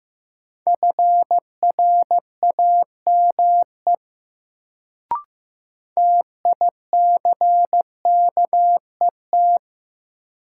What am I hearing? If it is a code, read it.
Morse code, heard as a keyed tone, about 15 wpm, 700 Hz. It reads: FRAME TICKET
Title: Sentences from Top 1400 Words - New Word Drill Alt - 15wpm